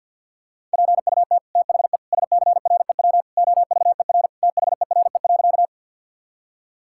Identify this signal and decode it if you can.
Morse code, heard as a keyed tone, about 50 wpm, 700 Hz. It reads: PUT THE SCREW OVER THERE.
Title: Sentences from Top 1600 Words 50wpm